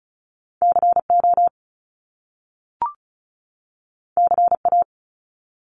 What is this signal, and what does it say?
Morse code, heard as a keyed tone, about 35 wpm, 700 Hz. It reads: CO CA
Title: US States 35wpm